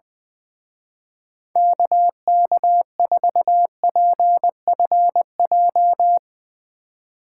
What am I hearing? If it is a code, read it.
Morse code, heard as a keyed tone, about 20 wpm, 700 Hz. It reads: KK4PFJ
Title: Speed Racing - USA 2x3 Call Signs 20wpm